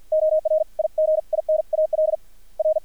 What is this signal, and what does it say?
Morse code, heard as a keyed tone, about 50 wpm, spaced as for 42 wpm, 630 Hz. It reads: ORIGINAL R
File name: archived / CRSnet